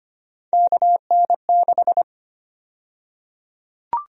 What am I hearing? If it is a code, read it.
Morse code, heard as a keyed tone, about 25 wpm, 700 Hz. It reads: KN6
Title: USA Call Sign Prefixes - Repeated 2x - 25wpm